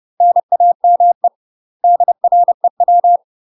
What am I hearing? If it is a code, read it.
Morse code, heard as a keyed tone, about 30 wpm, 700 Hz. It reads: NAME DREW
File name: UK MASTER QSOs TRAINING / 30WPM / UK MASTER QSOs 0051